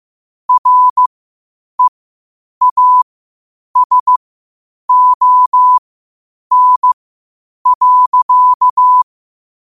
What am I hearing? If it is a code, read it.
Morse code, heard as a keyed tone, about 15 wpm, spaced as for 8 wpm, 1000 Hz. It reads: REASON.
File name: qotdfiles / 8wpm